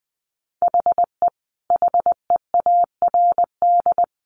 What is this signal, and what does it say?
Morse code, heard as a keyed tone, about 20 wpm, 700 Hz. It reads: HE HEARD
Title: A Sound Of Thunder 20wpm